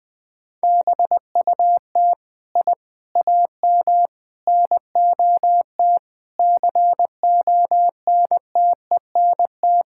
Morse code, heard as a keyed tone, about 20 wpm, 700 Hz. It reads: BUT I AM NOT CONTENT
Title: Sentences from Top 1000 Words - Review Part 6 - 20wpm